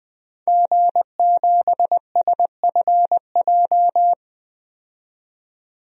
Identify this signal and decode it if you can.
Morse code, heard as a keyed tone, about 20 wpm, 700 Hz. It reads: G7SFJ
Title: DXCC-100 Call Signs - 5 Characters - 20wpm